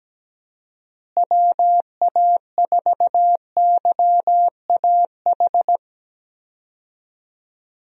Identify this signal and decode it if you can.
Morse code, heard as a keyed tone, about 17 wpm, 700 Hz. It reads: WA4YAH
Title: USA Call Signs - 6 Characters - Repeated 2x - 17wpm